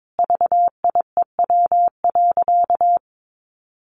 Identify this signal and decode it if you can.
Morse code, heard as a keyed tone, about 22 wpm, 700 Hz. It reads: VIEW.
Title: Sentences from Top 1400 Words 22wpm